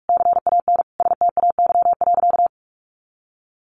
Morse code, heard as a keyed tone, about 44 wpm, 700 Hz. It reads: CAN STAY.
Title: Sentences from Top 1000 Words 45wpm